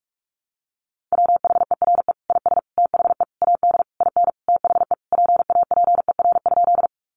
Morse code, heard as a keyed tone, about 44 wpm, 700 Hz. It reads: WHERE IS THE AD IN THE PAPER?
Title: Sentences from Top 1300 Words 45wpm